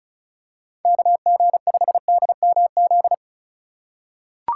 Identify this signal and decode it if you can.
Morse code, heard as a keyed tone, about 34 wpm, 690 Hz. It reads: KG5DMZ
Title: USA Call Signs - 6 Characters - 3x Character Spacing - 35wpm